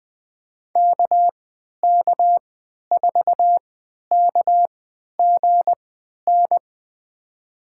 Morse code, heard as a keyed tone, about 20 wpm, spaced as for 11 wpm, 700 Hz. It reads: KK4KGN
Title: USA Call Signs - 3x Character Spacing - 20wpm